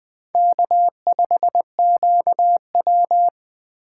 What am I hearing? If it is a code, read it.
Morse code, heard as a keyed tone, about 24 wpm, 700 Hz. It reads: K5QW
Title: Speed Racing - USA 1x2 Call Signs 20wpm